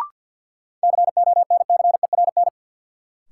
Morse code, heard as a keyed tone, about 52 wpm, 700 Hz. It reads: CONCERN
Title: Sentences from Top 1000 Words - New Word Drill - 50wpm